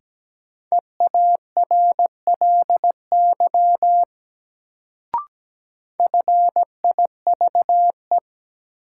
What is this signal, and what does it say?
Morse code, heard as a keyed tone, about 17 wpm, 700 Hz. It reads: EARLY FIVE
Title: Sentences from Top 400 Words - New Word Drill - 17wpm